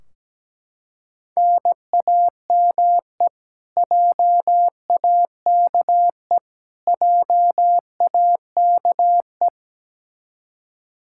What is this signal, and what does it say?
Morse code, heard as a keyed tone, about 17 wpm, 700 Hz. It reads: NAME JAKE JAKE
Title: Practice QSO - Exchange 1 - Beginning - 17wpm